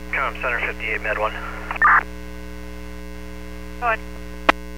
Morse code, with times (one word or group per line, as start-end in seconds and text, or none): none